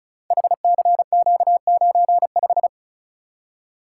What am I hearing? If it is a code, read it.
Morse code, heard as a keyed tone, about 36 wpm, 690 Hz. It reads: HCQ95
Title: Code Groups - 5 Letter:Numbers - 35wpm